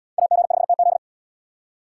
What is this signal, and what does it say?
Morse code, heard as a keyed tone, about 72 wpm, 700 Hz. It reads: AG5TZ